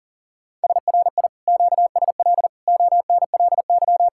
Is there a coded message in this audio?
Yes: Morse code from 0.6 to 4.1 s, heard as a keyed tone, about 52 wpm, 700 Hz. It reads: SRI QSL ONLY